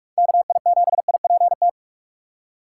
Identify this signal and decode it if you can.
Morse code, heard as a keyed tone, about 44 wpm, 700 Hz. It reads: KI7IPT
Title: USA Call Signs - 6 Characters - 3x Character Spacing - 45wpm